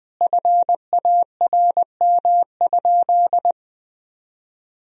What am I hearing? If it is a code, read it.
Morse code, heard as a keyed tone, about 20 wpm, 700 Hz. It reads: FARM?
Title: Sentences from Top 1000 Words - Review Part 2 - 20wpm